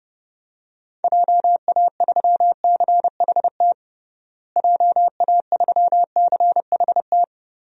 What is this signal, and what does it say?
Morse code, heard as a keyed tone, about 30 wpm, 700 Hz. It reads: JA3CHT JA3CHT
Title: DXCC-100 Call Signs - 6 Characters - Repeated 3x - 30wpm